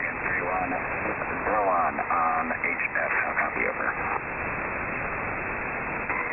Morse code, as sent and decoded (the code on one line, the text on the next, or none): none